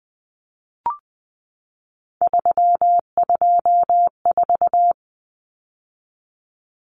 Morse code, heard as a keyed tone, about 20 wpm, 700 Hz. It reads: 324
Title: Numbers - 3-Digit 20wpm